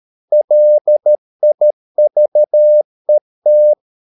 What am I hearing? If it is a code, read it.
Morse code, heard as a keyed tone, about 13 wpm, 600 Hz. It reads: LIVET
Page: Ferie 13 wpm